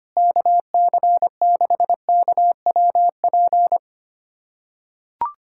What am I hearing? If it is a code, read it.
Morse code, heard as a keyed tone, about 25 wpm, 700 Hz. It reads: KC6KWP
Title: USA Callsign DE Callsign - 6 Characters - 25wpm